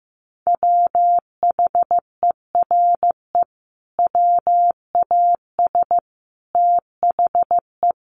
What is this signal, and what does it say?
Morse code, heard as a keyed tone, about 15 wpm, 700 Hz. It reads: WHERE WAS THE
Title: Sentences from Top 1000 Words 15wpm